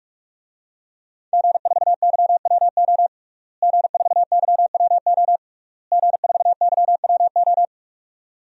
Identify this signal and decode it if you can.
Morse code, heard as a keyed tone, about 45 wpm, 700 Hz. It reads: G4YWO G4YWO G4YWO
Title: DXCC-100 Call Signs - 5 Characters - Repeated 3x - 45wpm